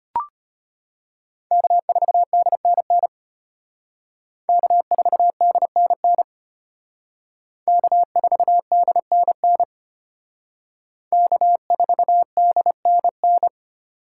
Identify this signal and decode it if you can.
Morse code, heard as a keyed tone, about 32 wpm, 700 Hz. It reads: K4DNN K4DNN K4DNN K4DNN